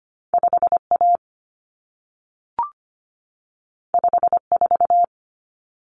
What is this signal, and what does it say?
Morse code, heard as a keyed tone, about 25 wpm, 700 Hz. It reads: 5A 54
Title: Common Cut Numbers - 2-Digit 25wpm